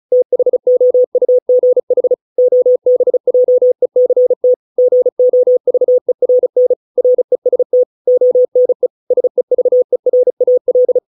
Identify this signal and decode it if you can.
Morse code, heard as a keyed tone, about 35 wpm, 500 Hz. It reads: THOUGH OBJECT GOVERN REST ONE SEVERAL